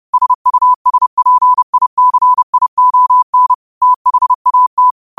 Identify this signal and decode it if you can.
Morse code, heard as a keyed tone, about 30 wpm, 1000 Hz. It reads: SUSPICION THAT
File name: qotdfiles / 30wpm